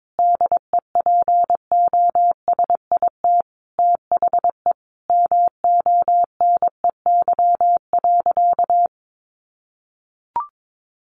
Morse code, heard as a keyed tone, about 22 wpm, 700 Hz. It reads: DEPOSIT THE MONEY.
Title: Sentences from Top 1300 Words 22wpm